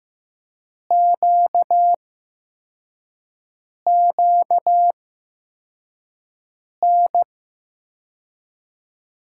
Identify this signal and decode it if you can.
Morse code, heard as a keyed tone, about 15 wpm, spaced as for 4 wpm, 700 Hz. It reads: QQN
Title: USA Call Signs - 6 Characters - 8x Character Spacing - 15wpm